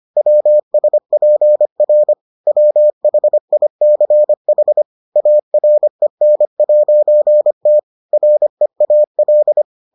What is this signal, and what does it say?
Morse code, heard as a keyed tone, about 25 wpm, 600 Hz. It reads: WSPR WHICH AREN'T REAL